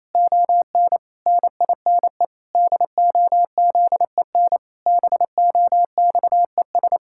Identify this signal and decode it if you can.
Morse code, heard as a keyed tone, about 28 wpm, 700 Hz. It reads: ON NINE DOZEN BOXES
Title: Pangram Challenge v2 - 28wpm